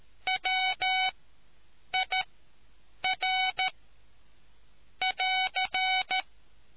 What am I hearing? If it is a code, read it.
Morse code, heard as a keyed tone, about 13 wpm, spaced as for 7 wpm, 750 Hz. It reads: WIR <AR>